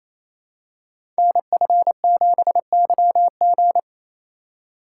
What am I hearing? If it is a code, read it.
Morse code, heard as a keyed tone, about 27 wpm, 700 Hz. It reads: NF7YG
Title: Code Groups - 5 Letter:Numbers - 28wpm